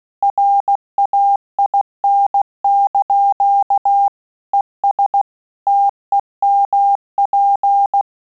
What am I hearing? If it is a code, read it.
Morse code, heard as a keyed tone, about 16 wpm, 800 Hz. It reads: RAIN) ES TEMP
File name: QSO CW 75 CPM